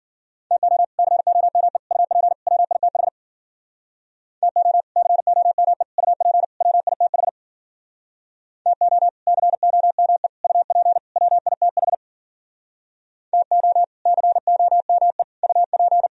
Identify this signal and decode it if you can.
Morse code, heard as a keyed tone, about 56 wpm, 700 Hz. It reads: TO COME UP WITH TO COME UP WITH TO COME UP WITH TO COME UP